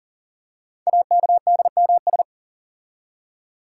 Morse code, heard as a keyed tone, about 43 wpm, 700 Hz. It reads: AKDMS
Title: Code Groups - 5 Letters - 40wpm